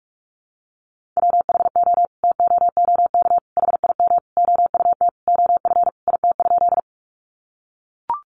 Morse code, heard as a keyed tone, about 44 wpm, 700 Hz. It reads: WHO TOOK HIM OUT OF IT?
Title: Sentences from Top 300 Words 45wpm